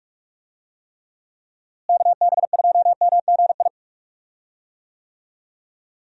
Morse code, heard as a keyed tone, about 46 wpm, 700 Hz. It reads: KB2MGI
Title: USA Call Signs - 6 Characters - Repeated 2x - 45wpm